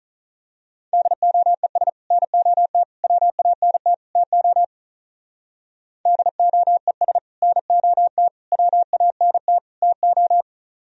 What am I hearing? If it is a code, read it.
Morse code, heard as a keyed tone, about 38 wpm, 700 Hz. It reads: DOES NOT WANT TO DOES NOT WANT TO
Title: Speed-Racing - Sets of 4 Words 35wpm